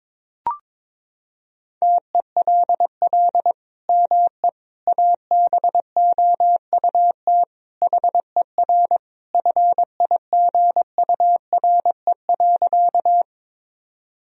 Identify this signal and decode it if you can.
Morse code, heard as a keyed tone, about 22 wpm, 700 Hz. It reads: TELL ME ABOUT HER FIGURE.
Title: Sentences from Top 1000 Words - Review Part 2 - 22wpm